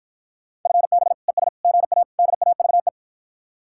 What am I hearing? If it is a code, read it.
Morse code, heard as a keyed tone, about 54 wpm, 700 Hz. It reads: FB ES GA DAVE